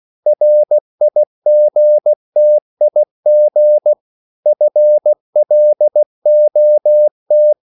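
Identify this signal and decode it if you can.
Morse code, heard as a keyed tone, about 16 wpm, 600 Hz. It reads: RIGTIG FLOT
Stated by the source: Slow (QRS) Morse code at 16 wpm, from CW med Gnister.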